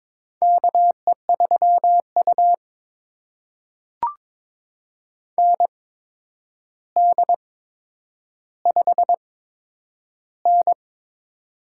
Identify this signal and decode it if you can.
Morse code, heard as a keyed tone, about 22 wpm, spaced as for 9 wpm, 700 Hz. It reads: KE3U ND5N
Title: USA Call Signs - 4 Characters - 8x Character Spacing - 22wpm